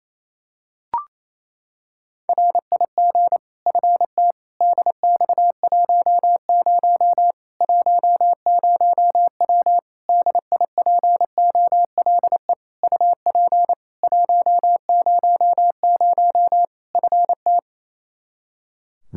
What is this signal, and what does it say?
Morse code, heard as a keyed tone, about 27 wpm, 700 Hz. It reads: RIG FT DX10 10W DIPOLE UP 100 FT
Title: Practice QSO - Exchange 2 - Info - 28wpm